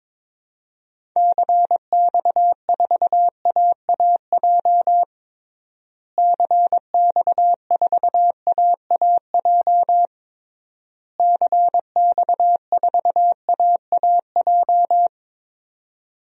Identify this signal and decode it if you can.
Morse code, heard as a keyed tone, about 22 wpm, 700 Hz. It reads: CX4AAJ CX4AAJ CX4AAJ
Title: DXCC-100 Call Signs - 6 Characters - Repeated 3x - 22wpm